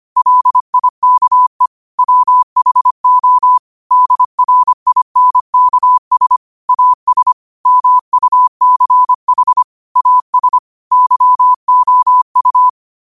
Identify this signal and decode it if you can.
Morse code, heard as a keyed tone, about 25 wpm, 1000 Hz. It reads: LIKE WHO DRINKS AS MUCH AS YOU